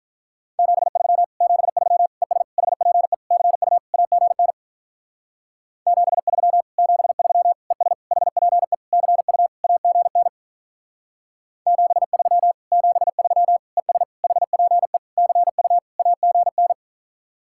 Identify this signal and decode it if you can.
Morse code, heard as a keyed tone, about 46 wpm, 700 Hz. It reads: 73 73 ES HPE CU AGN 73 73 ES HPE CU AGN 73 73 ES HPE CU AGN